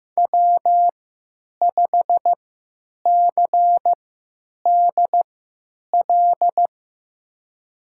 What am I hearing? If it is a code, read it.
Morse code, heard as a keyed tone, about 16 wpm, spaced as for 9 wpm, 700 Hz. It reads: W5CDL
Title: USA Call Signs - 5 Characters - 3x Character Spacing - 15wpm